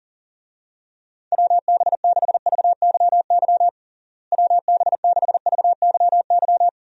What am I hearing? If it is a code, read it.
Morse code, heard as a keyed tone, about 40 wpm, 700 Hz. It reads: WB6VYY WB6VYY
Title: USA Call Signs - 6 Characters - Repeated 2x - 40wpm